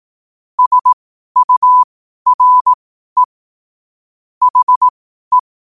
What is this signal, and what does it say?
Morse code, heard as a keyed tone, about 18 wpm, spaced as for 12 wpm, 1000 Hz. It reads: SURE HE
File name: qotdfiles / 12wpm